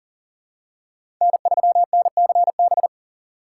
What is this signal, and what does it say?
Morse code, heard as a keyed tone, about 40 wpm, 700 Hz. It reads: N3NCB
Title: USA Call Signs - 5 Characters - 4x Character Spacing - 40wpm